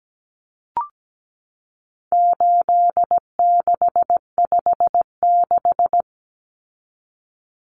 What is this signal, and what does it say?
Morse code, heard as a keyed tone, about 17 wpm, 700 Hz. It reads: 8656
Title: Common Cut Numbers - 4-Digit 17wpm